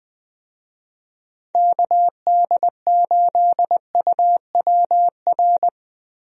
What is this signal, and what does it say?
Morse code, heard as a keyed tone, about 20 wpm, 700 Hz. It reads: KD8UWR
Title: USA Call Signs - 6 Characters - Repeated 2x - 20wpm